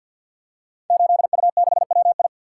Morse code, heard as a keyed tone, about 50 wpm, 700 Hz. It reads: 8U6WI
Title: Code Groups - 5 Letter:Numbers - 50wpm